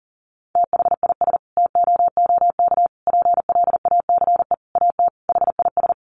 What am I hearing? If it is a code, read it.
Morse code, heard as a keyed tone, about 37 wpm, 700 Hz. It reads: THIS TOOK PLACE AT HIS
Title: Sentences from Top 300 Words 40wpm